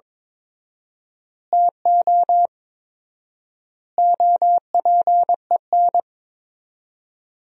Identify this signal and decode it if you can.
Morse code, heard as a keyed tone, about 22 wpm, 700 Hz. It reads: TO OPEN
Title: 4x Word-Spacing - Sentences from Top 300 Words - 22wpm